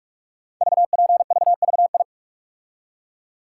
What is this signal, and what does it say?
Morse code, heard as a keyed tone, about 44 wpm, 700 Hz. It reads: VPVVI
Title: Code Groups - 5 Letters - 45wpm